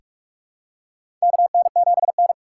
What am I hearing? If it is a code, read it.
Morse code, heard as a keyed tone, about 47 wpm, 690 Hz. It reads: KN7N
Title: USA Call Signs - 4 Characters - 8x Character Spacing - 45wpm